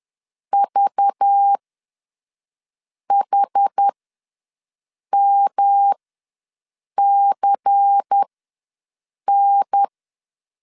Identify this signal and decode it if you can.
Morse code, heard as a keyed tone, about 10 wpm, spaced as for 5 wpm, 800 Hz. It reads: VHMCN
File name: LEZIONI / 5 LBVC / 5 LBVC ws 10wpm 3733